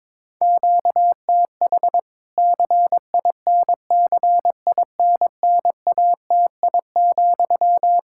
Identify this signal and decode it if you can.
Morse code, heard as a keyed tone, about 22 wpm, 700 Hz. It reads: QTH CINCINNATI,
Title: Practice QSO - Exchange 1 - Beginning - 22wpm